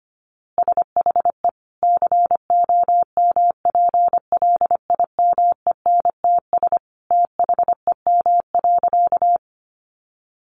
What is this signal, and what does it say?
Morse code, heard as a keyed tone, about 25 wpm, 700 Hz. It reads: SHE COMPLIMENTS THEM.
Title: Easy Phrases 25wpm